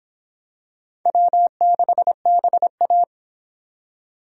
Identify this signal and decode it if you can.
Morse code, heard as a keyed tone, about 25 wpm, 700 Hz. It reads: W6BA